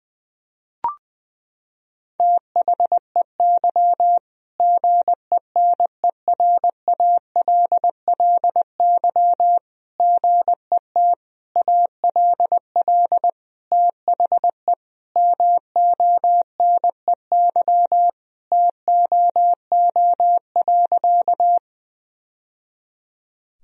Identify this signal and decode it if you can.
Morse code, heard as a keyed tone, about 20 wpm, 700 Hz. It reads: THEY GENERALLY GET ALL THE MONEY TOO.
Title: Sentences from Top 1000 Words - Review Part 4 - 20wpm